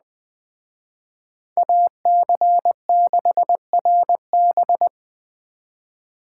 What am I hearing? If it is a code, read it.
Morse code, heard as a keyed tone, about 20 wpm, 700 Hz. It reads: AC6RB